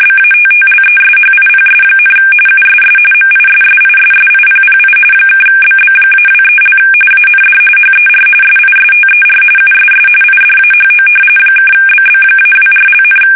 RTTY, about 45.45 baud. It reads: BATTLESHIPS, USS MISSISSIPPI (BB-23) AND HER SISTER USS IDAHO (BB-24),
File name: fsk850 / battleships